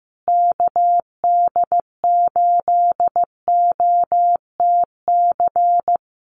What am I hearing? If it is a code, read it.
Morse code, heard as a keyed tone, about 14 wpm, 700 Hz. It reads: KD8OTC
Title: USA Call Signs Repeated 3x 15wpm